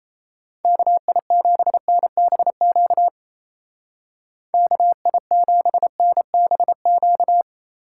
Morse code, heard as a keyed tone, about 34 wpm, 700 Hz. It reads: KI7NBQ KI7NBQ
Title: Speed Racing - USA 2x3 Call Signs 28wpm